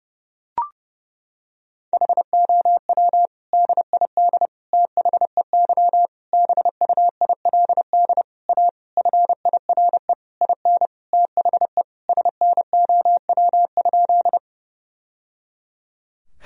Morse code, heard as a keyed tone, about 30 wpm, 700 Hz. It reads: HOW DID THEY BUILD A FIRE IN THE SNOW?
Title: Sentences from Top 1000 Words - Review Part 3 - 30wpm